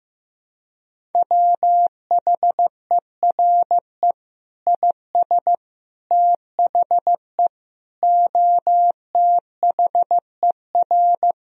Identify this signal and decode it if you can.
Morse code, heard as a keyed tone, about 16 wpm, 700 Hz. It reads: WHERE IS THE OTHER
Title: Sentences from Top 1000 Words - Review Part 4 - 15wpm